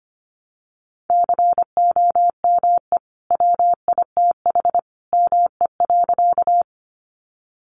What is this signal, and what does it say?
Morse code, heard as a keyed tone, about 25 wpm, 700 Hz. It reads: COME WITH ME.
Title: Sentences from Top 200 Words 25wpm